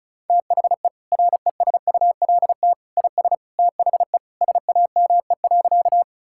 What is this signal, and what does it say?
Morse code, heard as a keyed tone, about 36 wpm, 700 Hz. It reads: THE RESULT IS THE SAME.
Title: Sentences from Top 1000 Words - Review Part 5 - 35wpm